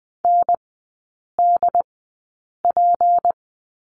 Morse code, heard as a keyed tone, about 20 wpm, spaced as for 8 wpm, 700 Hz. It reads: NDP
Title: 191 Lesson 31e - Letter v - Three Characters 20wpm